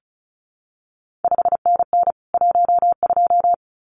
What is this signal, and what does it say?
Morse code, heard as a keyed tone, about 36 wpm, 700 Hz. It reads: 5NN 12
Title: CQ WW DX Contest Exchange Encore - 35wpm